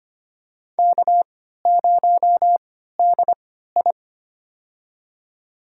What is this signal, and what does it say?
Morse code, heard as a keyed tone, about 25 wpm, spaced as for 14 wpm, 700 Hz. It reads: K0DI
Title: USA Call Signs - 4 Characters - 3x Character Spacing - 25wpm